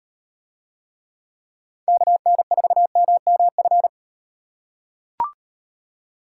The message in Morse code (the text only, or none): KN4MMF